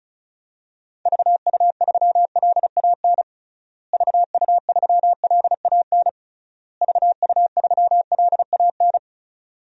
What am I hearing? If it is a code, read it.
Morse code, heard as a keyed tone, about 34 wpm, 700 Hz. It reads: VU3LAN VU3LAN VU3LAN
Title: DXCC-100 Call Signs - 6 Characters - Repeated 3x - 35wpm